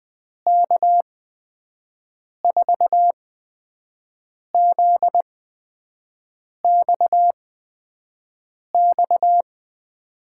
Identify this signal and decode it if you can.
Morse code, heard as a keyed tone, about 20 wpm, spaced as for 5 wpm, 700 Hz. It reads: K4ZXX
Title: USA Call Signs - 8x Character Spacing - 20wpm